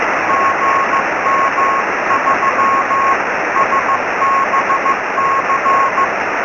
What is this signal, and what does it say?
Morse code, heard as a keyed tone, about 15 wpm, 1060 Hz, with noise. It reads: GM3SBC
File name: sm7lcb / gm3sbc 23cm